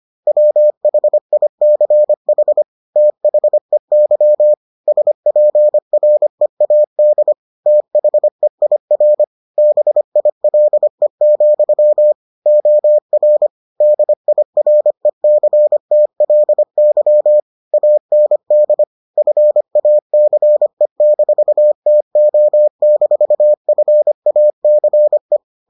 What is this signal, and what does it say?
Morse code, heard as a keyed tone, about 25 wpm, 600 Hz. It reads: WHICH THEY SPREAD THEIR BILE, OR DIRECTLY AND FACE-TO-FACE